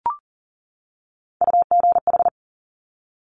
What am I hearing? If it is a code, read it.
Morse code, heard as a keyed tone, about 41 wpm, 690 Hz. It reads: UGH
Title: Top Three Letter Combinations in Words 40wpm